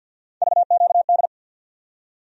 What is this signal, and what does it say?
Morse code, heard as a keyed tone, about 53 wpm, 700 Hz. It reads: VQD
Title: USA Call Sign Postfixes - 3 Letters 50wpm